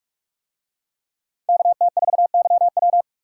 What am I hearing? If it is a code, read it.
Morse code, heard as a keyed tone, about 44 wpm, 700 Hz. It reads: KT4YW